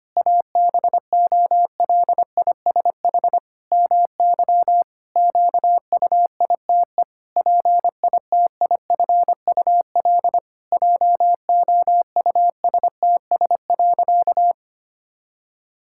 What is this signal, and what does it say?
Morse code, heard as a keyed tone, about 25 wpm, 700 Hz. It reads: ABOLISH MY QUITE PITIFUL JOUSTS.
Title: Pangram Challenge v2 - 25wpm